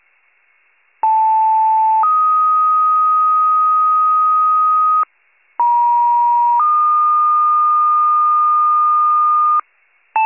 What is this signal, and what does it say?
Two-tone page, heard as a one-second tone, then a three-second tone, about 880 then 1250 Hz, about 950 then 1220 Hz. It is heard from 1.0 to 9.6 s.